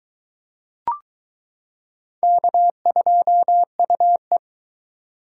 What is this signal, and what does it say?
Morse code, heard as a keyed tone, about 22 wpm, 700 Hz. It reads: K2UE